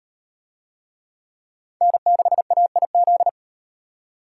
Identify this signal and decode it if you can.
Morse code, heard as a keyed tone, about 38 wpm, 700 Hz. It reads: N6AIZ